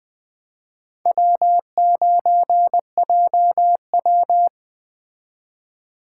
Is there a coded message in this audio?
Yes: Morse code from 1.1 to 4.5 s, heard as a keyed tone, about 20 wpm, 700 Hz. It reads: W9JW